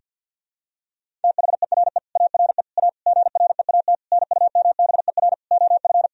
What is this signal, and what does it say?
Morse code, heard as a keyed tone, about 52 wpm, 700 Hz. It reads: THERE ARE A GREAT NUMBER OF
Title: Sentences from Top 1000 Words - Review Part 1 - 50wpm